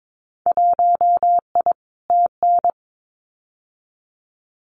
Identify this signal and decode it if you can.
Morse code, heard as a keyed tone, about 22 wpm, 700 Hz. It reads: 1I TN
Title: Winter Field Day Exchanges Encore 22wpm